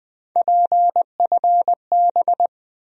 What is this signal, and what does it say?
Morse code, heard as a keyed tone, about 20 wpm, 700 Hz. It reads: PFB
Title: USA Call Sign Postfixes - 3 Letters 20wpm